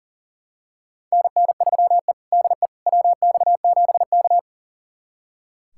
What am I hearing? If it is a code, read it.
Morse code, heard as a keyed tone, about 40 wpm, 700 Hz. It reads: NN3E DE WX7K
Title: USA Callsign DE Callsign - 4 Characters - 40wpm